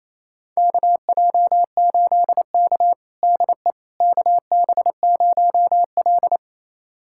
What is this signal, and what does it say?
Morse code, heard as a keyed tone, about 28 wpm, 700 Hz. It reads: KJ8K DE KB0L
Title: USA Callsign DE Callsign - 4 Characters - 28wpm